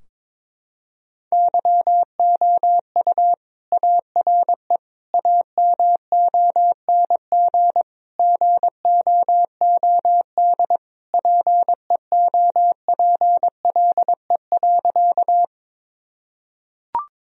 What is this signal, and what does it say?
Morse code, heard as a keyed tone, about 22 wpm, 700 Hz. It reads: YOU ARE AMONG GOOD PEOPLE.
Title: Sentences from Top 1000 Words - Review Part 3 - 22wpm